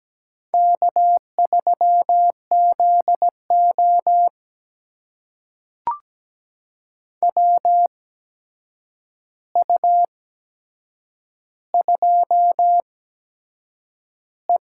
Morse code, heard as a keyed tone, about 17 wpm, spaced as for 7 wpm, 700 Hz. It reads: K3ZO WU2E
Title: USA Call Signs - 4 Characters - 8x Character Spacing - 17wpm